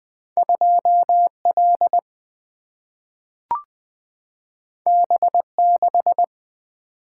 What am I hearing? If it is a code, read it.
Morse code, heard as a keyed tone, about 20 wpm, 700 Hz. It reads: 2 L B 6
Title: Single-Letter Single-Number 20wpm